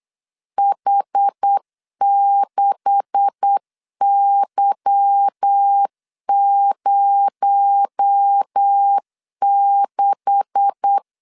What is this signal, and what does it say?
Morse code, heard as a keyed tone, about 8 wpm, 800 Hz. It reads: H6Y06
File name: LEZIONI / 9 286 8wpm 3425